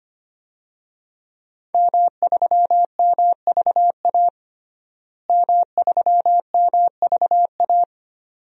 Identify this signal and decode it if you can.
Morse code, heard as a keyed tone, about 25 wpm, 700 Hz. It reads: M3MVA M3MVA
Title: DXCC-100 Call Signs - 5 Characters - Repeated 2x - 25wpm